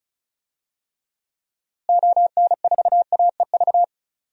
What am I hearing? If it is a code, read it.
Morse code, heard as a keyed tone, about 34 wpm, 700 Hz. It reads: ON4AEV
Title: DXCC-100 Call Signs - 6 Characters - Repeated 3x - 35wpm